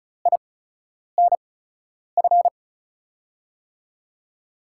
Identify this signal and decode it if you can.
Morse code, heard as a keyed tone, about 34 wpm, spaced as for 10 wpm, 700 Hz. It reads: INF
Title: USA Call Signs - 5 Characters - 8x Character Spacing - 35wpm